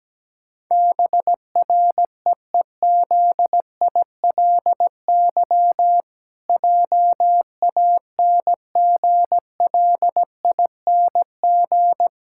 Morse code, heard as a keyed tone, about 17 wpm, 700 Hz. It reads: BREEZILY JANGLING
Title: Pangram Challenge v2 - 17wpm